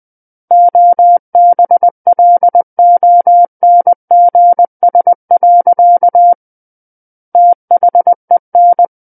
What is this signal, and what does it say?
Morse code, heard as a keyed tone, about 20 wpm, 700 Hz. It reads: OBLONGS. THEN
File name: 20 WPM